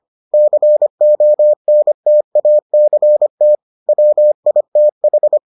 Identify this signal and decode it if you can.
Morse code, heard as a keyed tone, about 25 wpm, 600 Hz. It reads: CONTACT WITH